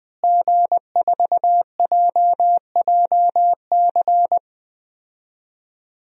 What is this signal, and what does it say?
Morse code, heard as a keyed tone, about 20 wpm, 700 Hz. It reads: G4JJC
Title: DXCC-100 Call Signs - 5 Characters - Repeated 2x - 20wpm